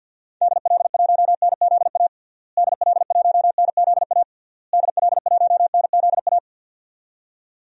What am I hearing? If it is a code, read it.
Morse code, heard as a keyed tone, about 50 wpm, 700 Hz. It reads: DL1NZA DL1NZA DL1NZA
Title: DXCC-100 Call Signs - 6 Characters - Repeated 3x - 50wpm